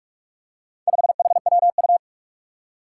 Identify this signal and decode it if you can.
Morse code, heard as a keyed tone, about 44 wpm, 700 Hz. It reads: 5HWU
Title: Four Challenging Letter-Number Combinations 45wpm